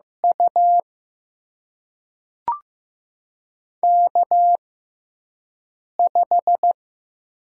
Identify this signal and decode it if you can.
Morse code, heard as a keyed tone, about 15 wpm, spaced as for 5 wpm, 700 Hz. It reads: U K5
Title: USA Call Signs - 4 Characters - 6x Character Spacing - 15wpm